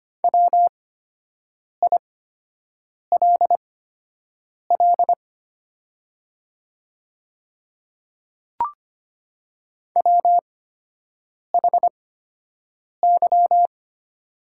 Morse code, heard as a keyed tone, about 25 wpm, spaced as for 7 wpm, 700 Hz. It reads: WILL WHY
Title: 8x Character Spacing - Top 300 Words - 25wpm